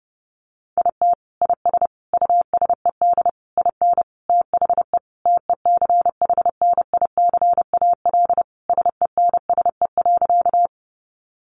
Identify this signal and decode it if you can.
Morse code, heard as a keyed tone, about 30 wpm, 700 Hz. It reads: IT IS USED IN THE TECHNICAL SENSE.
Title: Sentences from Top 1100 Words 30wpm